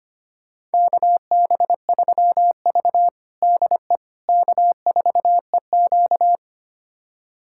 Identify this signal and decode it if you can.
Morse code, heard as a keyed tone, about 25 wpm, 700 Hz. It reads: KB3V DE K4EQ
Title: USA Callsign DE Callsign - 4 Characters - 25wpm